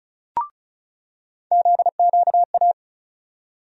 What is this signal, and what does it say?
Morse code, heard as a keyed tone, about 36 wpm, 700 Hz. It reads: ZQA
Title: USA Call Sign Postfixes - 3 Letters 35wpm